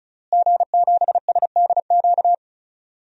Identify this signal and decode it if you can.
Morse code, heard as a keyed tone, about 34 wpm, 700 Hz. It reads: G7SDQ
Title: DXCC-100 Call Signs - 5 Characters - Repeated 3x - 35wpm